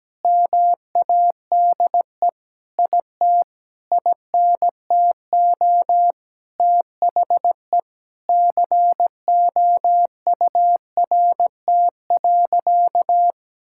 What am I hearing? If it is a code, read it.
Morse code, heard as a keyed tone, about 17 wpm, 700 Hz. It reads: MADE IT INTO THE COURT.
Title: Sentences from Top 1600 Words 17wpm